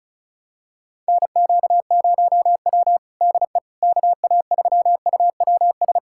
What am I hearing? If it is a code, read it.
Morse code, heard as a keyed tone, about 36 wpm, 700 Hz. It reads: NQ0W DE KA3UWS
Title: USA Callsign DE Callsign 35wpm